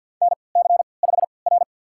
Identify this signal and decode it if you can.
Morse code, heard as a keyed tone, about 53 wpm, spaced as for 29 wpm, 700 Hz. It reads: NC5R
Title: USA Call Signs - 4 Characters - 3x Character Spacing - 50wpm